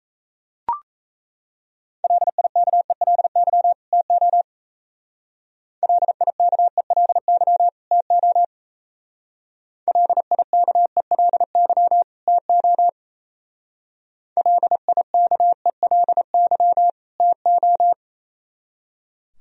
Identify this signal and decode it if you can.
Morse code, heard as a keyed tone, about 34 wpm, 700 Hz. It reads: LIKELY TO LIKELY TO LIKELY TO LIKELY TO
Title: Speed-Racing - Sets of 2 Words 28wpm